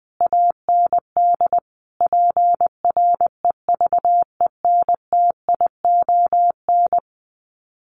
Morse code, heard as a keyed tone, about 20 wpm, 700 Hz. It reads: AND PREVENTION
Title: Sets of 5 Words Encore2 20wpm